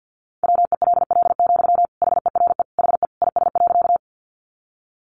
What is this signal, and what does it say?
Morse code, heard as a keyed tone, about 50 wpm, 700 Hz. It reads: WELL, HERE HE IS.
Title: Sentences from Top 200 Words 50wpm